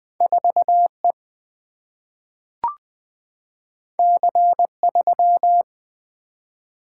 Morse code, heard as a keyed tone, about 20 wpm, 700 Hz. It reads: 4 E C 3
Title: Single-Letter Single-Number 20wpm